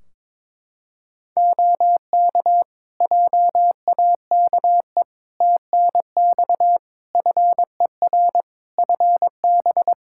Morse code, heard as a keyed tone, about 22 wpm, 700 Hz. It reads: OK JAKE TNX FER FB
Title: Practice QSO - Exchange 4 - Ending - 22wpm